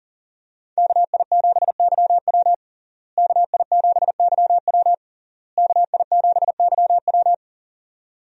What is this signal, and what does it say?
Morse code, heard as a keyed tone, about 40 wpm, 700 Hz. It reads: KI7YW KI7YW KI7YW
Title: USA Call Signs - 5 Characters - Repeated 3x - 40wpm